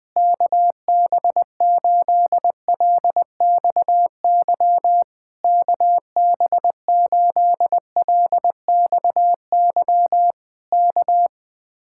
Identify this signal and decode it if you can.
Morse code, heard as a keyed tone, about 20 wpm, 700 Hz. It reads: KB8LXY KB8LXY K
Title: Speed-Racing Common QSO Phrases Encore - 20wpm